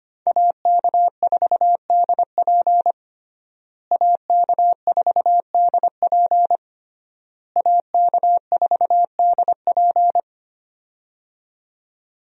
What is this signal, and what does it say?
Morse code, heard as a keyed tone, about 25 wpm, 700 Hz. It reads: AK4DP AK4DP AK4DP
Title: USA Call Signs - 5 Characters - Repeated 3x - 25wpm